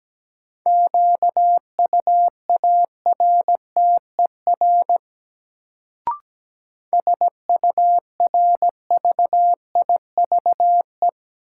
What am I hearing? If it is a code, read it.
Morse code, heard as a keyed tone, about 17 wpm, 700 Hz. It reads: QUARTER SURVIVE
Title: Sentences from Top 1300 Words - New Word Drill - 17wpm